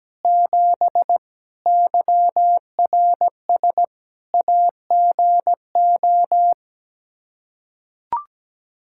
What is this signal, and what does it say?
Morse code, heard as a keyed tone, about 17 wpm, 700 Hz. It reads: 7 YRS AGO
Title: Practice QSO - Exchange 3 - More Info - 17wpm